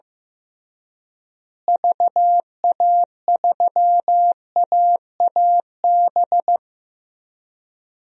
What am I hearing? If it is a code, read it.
Morse code, heard as a keyed tone, about 15 wpm, 700 Hz. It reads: VA3AAB
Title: DXCC-100 Call Signs - 6 Characters - Repeated 3x - 15wpm